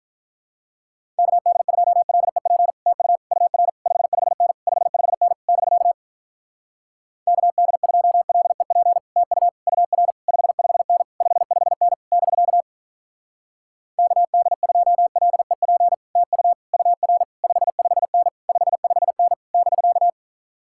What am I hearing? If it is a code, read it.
Morse code, heard as a keyed tone, about 47 wpm, 700 Hz. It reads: KD2LEP TU UR 55N 55N <BK> KD2LEP TU UR 55N 55N <BK> KD2LEP TU UR 55N 55N <BK>